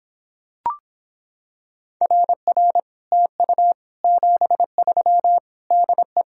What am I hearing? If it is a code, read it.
Morse code, heard as a keyed tone, about 26 wpm, 700 Hz. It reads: RR TU 73 DE